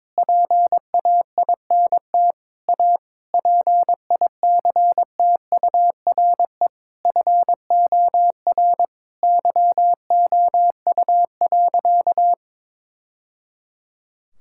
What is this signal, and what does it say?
Morse code, heard as a keyed tone, about 22 wpm, 700 Hz. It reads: PAINT A PICTURE FOR YOU.
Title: Sentences from Top 1000 Words - Review Part 6 - 22wpm